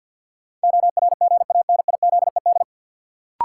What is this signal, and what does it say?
Morse code, heard as a keyed tone, about 53 wpm, 700 Hz. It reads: ORGANIZED
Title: Sentences from Top 1300 Words - New Word Drill Alt - 50wpm